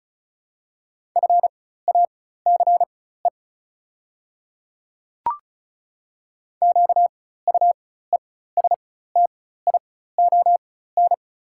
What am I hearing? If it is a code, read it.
Morse code, heard as a keyed tone, about 34 wpm, spaced as for 16 wpm, 690 Hz. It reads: FACE QUESTION
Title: 4x Character Spacing - Top 500 Words - 35wpm